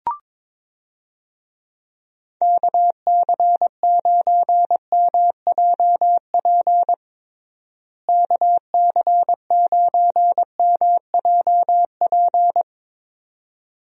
Morse code, heard as a keyed tone, about 22 wpm, 700 Hz. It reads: KC9MJP KC9MJP
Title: USA Call Signs - 6 Characters - Repeated 2x - 22wpm